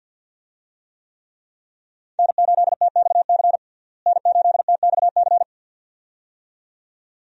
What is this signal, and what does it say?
Morse code, heard as a keyed tone, about 50 wpm, 700 Hz. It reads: N8TXC N8TXC
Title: USA Call Signs - 5 Characters - Repeated 2x - 50wpm